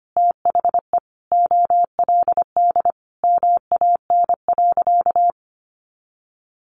Morse code, heard as a keyed tone, about 25 wpm, 700 Hz. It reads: THE OLD MAN.
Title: Sentences from Top 600 Words 25wpm